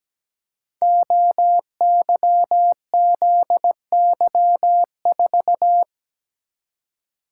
Morse code, heard as a keyed tone, about 17 wpm, 700 Hz. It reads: OYZY4
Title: Code Groups - 5 Letter:Numbers - 17wpm